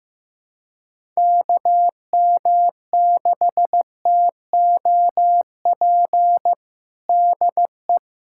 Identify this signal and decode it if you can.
Morse code, heard as a keyed tone, about 15 wpm, 700 Hz. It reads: KM6TOP DE
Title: USA Callsign DE Callsign - 6 Characters - 15wpm